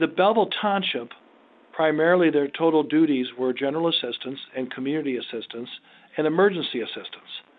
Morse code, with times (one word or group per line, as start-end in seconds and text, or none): none